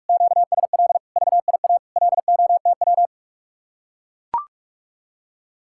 Morse code, heard as a keyed tone, about 44 wpm, 700 Hz. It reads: QSL VIA LOTW